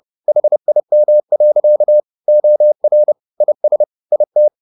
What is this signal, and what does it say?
Morse code, heard as a keyed tone, about 30 wpm, 600 Hz. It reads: HIM. OR IS IT